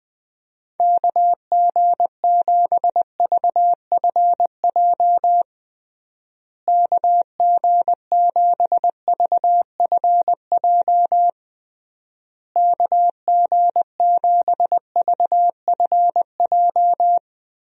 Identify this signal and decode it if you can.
Morse code, heard as a keyed tone, about 21 wpm, 700 Hz. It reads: KG7VFJ KG7VFJ KG7VFJ
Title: USA Call Signs - 6 Characters - Repeated 3x - 20wpm